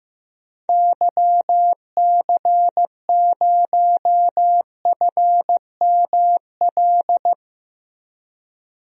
Morse code, heard as a keyed tone, about 15 wpm, 700 Hz. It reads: YC0FML
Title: DXCC-100 Call Signs - 6 Characters - 15wpm